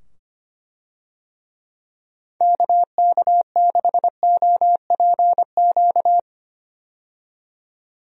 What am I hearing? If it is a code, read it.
Morse code, heard as a keyed tone, about 25 wpm, 700 Hz. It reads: KK6OPQ
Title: USA Call Signs - 6 Characters - 25wpm